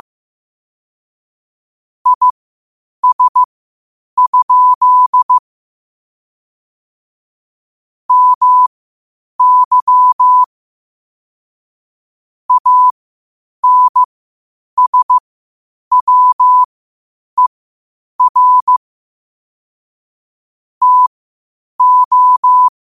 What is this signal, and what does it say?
Morse code, heard as a keyed tone, about 15 wpm, spaced as for 8 wpm, 1000 Hz. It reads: IS? MY ANSWER TO
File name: qotdfiles / 8wpm